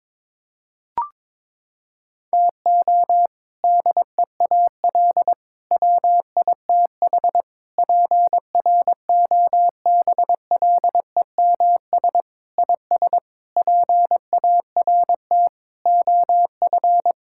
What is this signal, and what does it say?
Morse code, heard as a keyed tone, about 22 wpm, 700 Hz. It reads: TO DEAL WITH PROBLEMS IS PART OF
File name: practice / 22wpm / Sentences with 3-Word Phrases - 22wpm